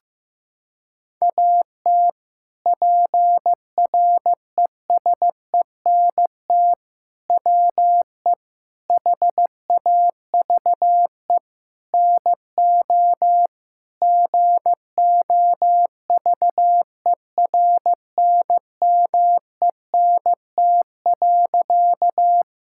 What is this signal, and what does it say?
Morse code, heard as a keyed tone, about 16 wpm, 700 Hz. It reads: AT PRESENT WE HAVE NO GOVERNMENT.
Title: Sentences from Top 1000 Words - Review Part 5 - 15wpm